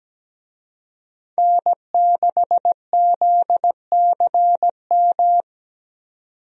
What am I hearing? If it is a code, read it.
Morse code, heard as a keyed tone, about 17 wpm, 700 Hz. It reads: N6ZCM
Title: USA Call Signs - 5 Characters - 17wpm